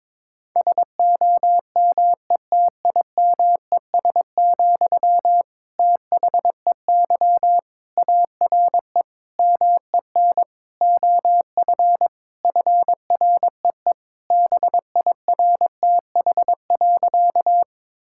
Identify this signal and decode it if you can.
Morse code, heard as a keyed tone, about 22 wpm, 700 Hz. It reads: SOMETIMES, THEY ARE MEN OF FREE BIRTH.
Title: Sentences from Top 1000 Words - Review Part 5 - 22wpm